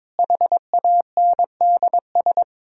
Morse code, heard as a keyed tone, about 22 wpm, 700 Hz. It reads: HANDS
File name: practice / 22wpm / Sets of 7 Words - 22wpm